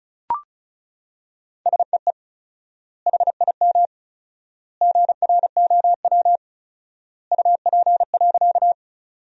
Morse code, heard as a keyed tone, about 36 wpm, 700 Hz. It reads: SEE HIM GROW UP.
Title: 4x Word-Spacing - Sentences from Top 300 Words - 35wpm